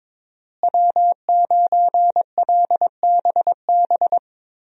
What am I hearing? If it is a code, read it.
Morse code, heard as a keyed tone, about 21 wpm, 700 Hz. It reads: W9LBB
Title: USA Call Signs - 5 Characters - Repeated 3x - 22wpm